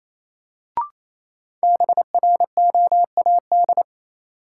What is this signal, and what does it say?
Morse code, heard as a keyed tone, about 28 wpm, 700 Hz. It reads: BROAD
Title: Sentences from Top 1000 Words - New Word Drill - 28wpm